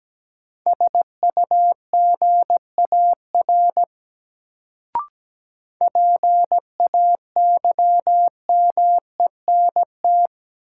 Morse code, heard as a keyed tone, about 17 wpm, 700 Hz. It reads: SUGAR PAYMENT
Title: Sentences from Top 1200 Words - New Word Drill - 17wpm